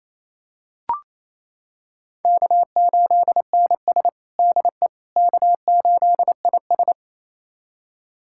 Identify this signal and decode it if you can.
Morse code, heard as a keyed tone, about 28 wpm, 700 Hz. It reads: K8NS DE K8IS
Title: USA Callsign DE Callsign - 4 Characters - 28wpm